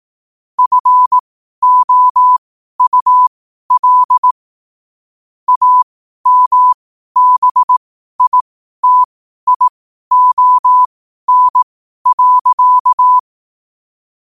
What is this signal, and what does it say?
Morse code, heard as a keyed tone, about 18 wpm, spaced as for 12 wpm, 1000 Hz. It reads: FOUL AMBITION.
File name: qotdfiles / 12wpm